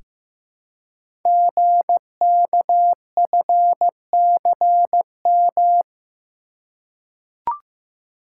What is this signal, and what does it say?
Morse code, heard as a keyed tone, about 15 wpm, 700 Hz. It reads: GKFCM
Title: Code Groups - 5 Letters - 15wpm